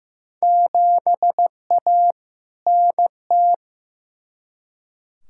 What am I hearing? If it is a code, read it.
Morse code, heard as a keyed tone, about 15 wpm, 700 Hz. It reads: 7A NT
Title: Field Day Exchanges Encore - 15wpm